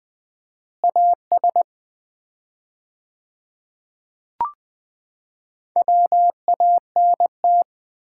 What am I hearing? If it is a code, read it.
Morse code, heard as a keyed tone, about 20 wpm, 700 Hz. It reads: AS WANT
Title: 4x Word-Spacing - Sets of 3 Words - 20wpm